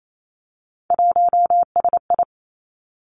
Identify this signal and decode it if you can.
Morse code, heard as a keyed tone, about 28 wpm, 700 Hz. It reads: 1SI
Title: Three Challenging Letter-Number Combinations 28wpm